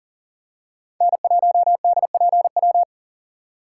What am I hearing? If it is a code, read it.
Morse code, heard as a keyed tone, about 40 wpm, 700 Hz. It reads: N1DPW
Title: USA Call Signs - 4x Character Spacing - 40wpm